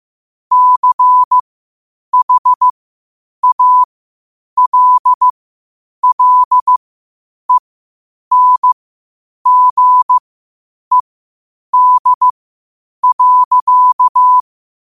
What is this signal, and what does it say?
Morse code, heard as a keyed tone, about 15 wpm, spaced as for 8 wpm, 1000 Hz. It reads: CHALLENGED.
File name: qotdfiles / 8wpm